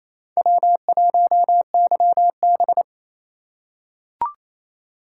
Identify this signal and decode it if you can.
Morse code, heard as a keyed tone, about 28 wpm, 700 Hz. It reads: W1YB
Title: USA Call Signs - 2x Character Spacing - 28wpm